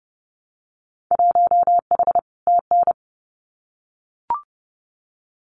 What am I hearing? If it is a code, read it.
Morse code, heard as a keyed tone, about 30 wpm, 700 Hz. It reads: 1H TN
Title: Winter Field Day Exchanges 30wpm